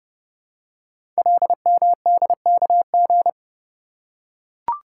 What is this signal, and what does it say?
Morse code, heard as a keyed tone, about 30 wpm, 700 Hz. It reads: LMDKG
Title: Code Groups - 5 Letters - 30wpm